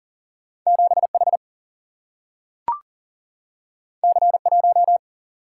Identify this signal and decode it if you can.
Morse code, heard as a keyed tone, about 39 wpm, 700 Hz. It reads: 7H C1
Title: Single-Letter Single-Number 40wpm